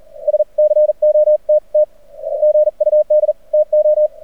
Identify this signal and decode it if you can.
Morse code, heard as a keyed tone, about 42 wpm, spaced as for 36 wpm, 590 Hz. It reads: SCOTT GUD TO
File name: archived / CRSnet